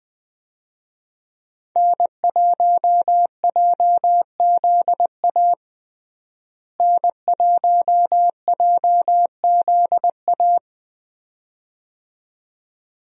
Morse code, heard as a keyed tone, about 21 wpm, 700 Hz. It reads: N1JZA N1JZA
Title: USA Call Signs - 5 Characters - Repeated 2x - 20wpm